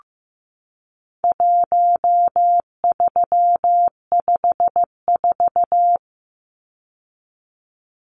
Morse code, heard as a keyed tone, about 15 wpm, 700 Hz. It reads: 1354
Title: Numbers - 4-Digit 15wpm